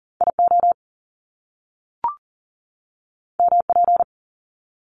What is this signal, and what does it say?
Morse code, heard as a keyed tone, about 38 wpm, 700 Hz. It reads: IO MP
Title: Top Two Letter Combinations in Words 40wpm